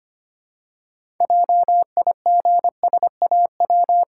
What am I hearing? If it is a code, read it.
Morse code, heard as a keyed tone, about 25 wpm, 700 Hz. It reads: JIGSAW
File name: practice / 25wpm / Difficult Words with Hints - 25wpm